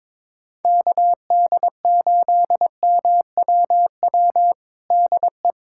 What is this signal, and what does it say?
Morse code, heard as a keyed tone, about 21 wpm, 700 Hz. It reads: KD8MWW DE
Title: USA Callsign DE Callsign - 6 Characters - 22wpm